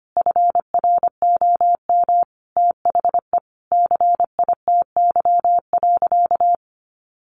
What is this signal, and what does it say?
Morse code, heard as a keyed tone, about 25 wpm, 700 Hz. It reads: FROM THE CITY.
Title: Sentences from Top 1200 Words 25wpm